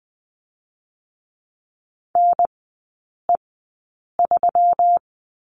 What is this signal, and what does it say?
Morse code, heard as a keyed tone, about 20 wpm, spaced as for 8 wpm, 700 Hz. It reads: NE3
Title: 121 Lesson 20e - Number 3 - Three Characters 20wpm